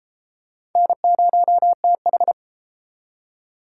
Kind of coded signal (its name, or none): Morse code